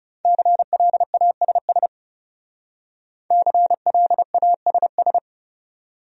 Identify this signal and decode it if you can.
Morse code, heard as a keyed tone, about 39 wpm, 700 Hz. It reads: CLASS CLASS